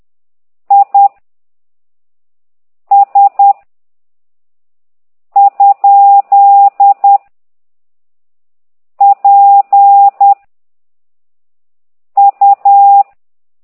Morse code, heard as a keyed tone, about 10 wpm, 800 Hz. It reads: I S ? P U
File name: ct2idw / exerc9